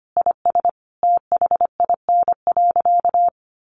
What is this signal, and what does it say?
Morse code, heard as a keyed tone, about 25 wpm, 700 Hz. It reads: IS THIN.
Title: Sentences from Top 1200 Words 25wpm